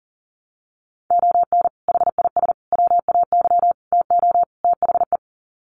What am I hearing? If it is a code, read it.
Morse code, heard as a keyed tone, about 37 wpm, 700 Hz. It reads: ON HIS WAY TO THE
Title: Sets of 5 Words Encore2 40wpm